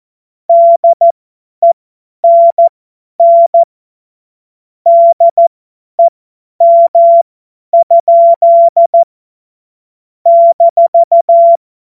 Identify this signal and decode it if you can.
Morse code, heard as a keyed tone, about 14 wpm, spaced as for 10 wpm, 680 Hz. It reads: DENN DEM? -
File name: Morsekurs / Witze / Witz 26 März 2024 50 F70